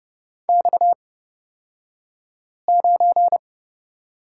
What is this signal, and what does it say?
Morse code, heard as a keyed tone, about 30 wpm, spaced as for 5 wpm, 700 Hz. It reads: X9
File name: Lesson 37d - Letter x - Two Characters 30wpm